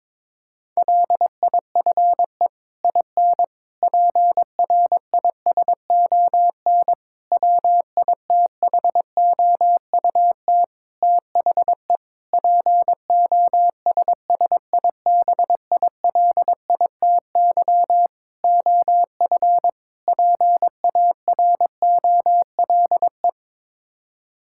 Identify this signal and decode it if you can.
Morse code, heard as a keyed tone, about 21 wpm, 700 Hz. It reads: LIFE IN PRISON WITHOUT THE POSSIBILITY OF PAROLE
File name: practice / 22wpm / Sets of 8 Words - 22wpm